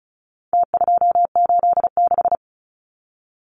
Morse code, heard as a keyed tone, about 34 wpm, 700 Hz. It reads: T286
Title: Numbers - Contest Style - 4-Digit 35wpm